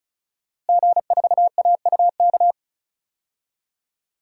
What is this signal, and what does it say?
Morse code, heard as a keyed tone, about 34 wpm, 700 Hz. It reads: G4AUK
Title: DXCC-100 Call Signs - 5 Characters - Repeated 3x - 35wpm